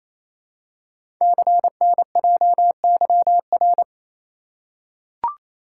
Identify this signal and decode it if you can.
Morse code, heard as a keyed tone, about 29 wpm, 700 Hz. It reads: CNJYR
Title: Code Groups - 5 Letters - 28wpm